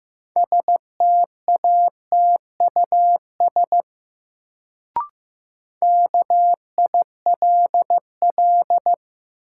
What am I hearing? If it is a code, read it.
Morse code, heard as a keyed tone, about 15 wpm, 700 Hz. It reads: STATUS KILL
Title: Sentences from Top 1400 Words - New Word Drill - 15wpm